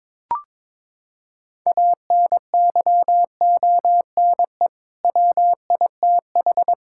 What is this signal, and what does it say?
Morse code, heard as a keyed tone, about 22 wpm, 700 Hz. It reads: ANYONE WITH
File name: practice / 22wpm / Sets of 7 Words - 22wpm